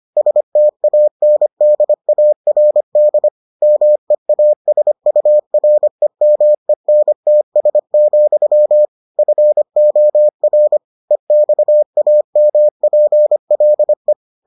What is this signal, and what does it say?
Morse code, heard as a keyed tone, about 25 wpm, 600 Hz. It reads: STANDARD MEASUREMENTS, FOR EXAMPLE